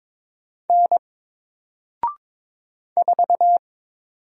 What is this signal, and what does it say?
Morse code, heard as a keyed tone, about 22 wpm, 700 Hz. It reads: N 4
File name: Single-Letter-Number-22wpm